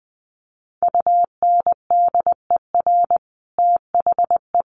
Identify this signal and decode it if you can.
Morse code, heard as a keyed tone, about 20 wpm, 700 Hz. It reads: UNDER THE
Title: Sets of 5 Words Encore2 20wpm